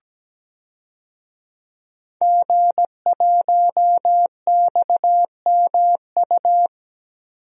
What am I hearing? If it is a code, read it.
Morse code, heard as a keyed tone, about 17 wpm, 700 Hz. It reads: G1XMU
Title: DXCC-100 Call Signs - 5 Characters - 17wpm